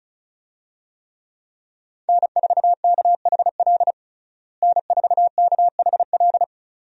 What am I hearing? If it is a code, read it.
Morse code, heard as a keyed tone, about 34 wpm, 700 Hz. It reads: N4KHL N4KHL
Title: USA Call Signs - 5 Characters - Repeated 2x - 35wpm